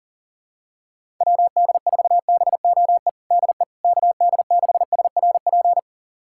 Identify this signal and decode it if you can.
Morse code, heard as a keyed tone, about 40 wpm, 700 Hz. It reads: WD4BOE DE KD6SRP
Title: USA Callsign DE Callsign 40wpm